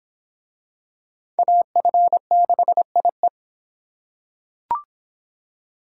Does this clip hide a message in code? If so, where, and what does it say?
Yes: Morse code from 1.4 to 3.3 s, heard as a keyed tone, about 24 wpm, 700 Hz. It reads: AF6IE